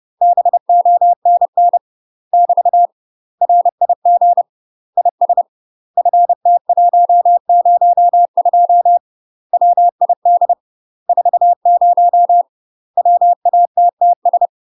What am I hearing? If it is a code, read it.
Morse code, heard as a keyed tone, about 30 wpm, 700 Hz. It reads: BONN <BT> RIG IS FT102 WID 40 WATTS
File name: UK MASTER QSOs TRAINING / 30WPM / UK MASTER QSOs 0031